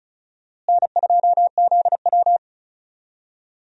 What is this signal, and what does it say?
Morse code, heard as a keyed tone, about 36 wpm, 700 Hz. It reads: N2ZW
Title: USA Call Signs - 4 Characters - Repeated 2x - 35wpm